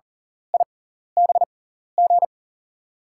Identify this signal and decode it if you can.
Morse code, heard as a keyed tone, about 40 wpm, spaced as for 14 wpm, 700 Hz. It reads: IBG
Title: USA Call Signs - 5 Characters - 6x Character Spacing - 40wpm